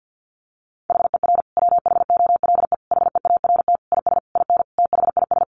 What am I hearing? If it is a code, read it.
Morse code, heard as a keyed tone, about 50 wpm, 700 Hz. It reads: HER WHOLE HEART IS IN THIS
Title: Sentences from Top 1500 Words 50wpm